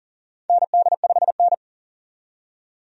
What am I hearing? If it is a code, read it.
Morse code, heard as a keyed tone, about 40 wpm, 700 Hz. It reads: ND5N
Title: USA Call Signs - 4 Characters - 40wpm